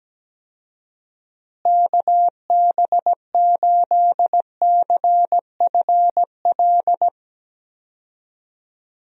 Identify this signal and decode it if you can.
Morse code, heard as a keyed tone, about 17 wpm, 700 Hz. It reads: KB8CFL
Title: USA Call Signs - 6 Characters - Repeated 2x - 17wpm